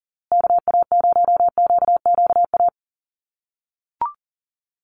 Morse code, heard as a keyed tone, about 40 wpm, 700 Hz. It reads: KA0QQA
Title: USA Call Signs Repeated 3x 40wpm